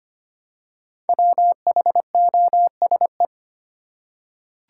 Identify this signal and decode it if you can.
Morse code, heard as a keyed tone, about 25 wpm, 700 Hz. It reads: WHOSE
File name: 5 Letter Words - First 3 Letters Given - 25wpm